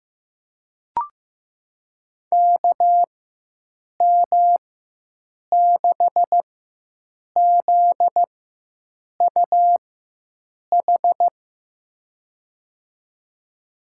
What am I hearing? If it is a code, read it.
Morse code, heard as a keyed tone, about 15 wpm, spaced as for 7 wpm, 700 Hz. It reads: KM6ZUH
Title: USA Call Signs - 6 Characters - 4x Character Spacing - 15wpm